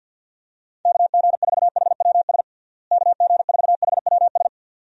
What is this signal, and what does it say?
Morse code, heard as a keyed tone, about 48 wpm, 700 Hz. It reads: KG4HWS KG4HWS
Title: USA Call Signs - 6 Characters - Repeated 2x - 50wpm